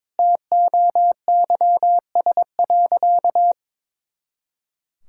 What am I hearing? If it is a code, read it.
Morse code, heard as a keyed tone, about 22 wpm, 700 Hz. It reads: TOYS.
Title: Pangram Challenge v2 - 22wpm